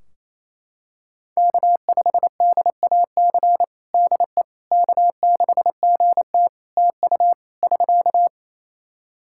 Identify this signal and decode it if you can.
Morse code, heard as a keyed tone, about 28 wpm, 700 Hz. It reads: K5DAC DE K6GT TU <SK>
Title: Practice QSO - Exchange 4 - Ending - 28wpm